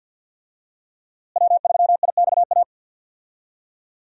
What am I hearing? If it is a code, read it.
Morse code, heard as a keyed tone, about 51 wpm, 700 Hz. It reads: W3IXA
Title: USA Call Signs - 5 Characters - Repeated 3x - 50wpm